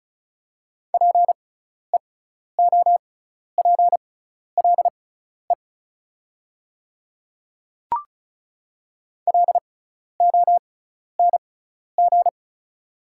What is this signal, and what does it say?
Morse code, heard as a keyed tone, about 34 wpm, spaced as for 12 wpm, 700 Hz. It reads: PEOPLE LONG
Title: 6x Character Spacing - Top 300 Words - 35wpm